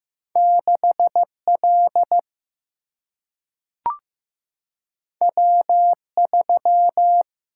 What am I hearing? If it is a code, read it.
Morse code, heard as a keyed tone, about 15 wpm, 700 Hz. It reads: 6 L W 3
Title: Single-Letter Single-Number 15wpm